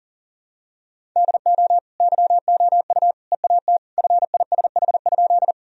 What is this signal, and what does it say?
Morse code, heard as a keyed tone, about 37 wpm, 700 Hz. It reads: DO YOU EAT FISH?
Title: Sentences from Top 1000 Words - Review Part 1 - 40wpm